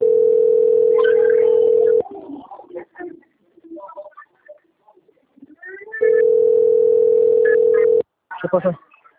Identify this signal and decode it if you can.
Ringback tone (North America), 440 + 480 Hz, with speech.
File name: out- - - - - .22345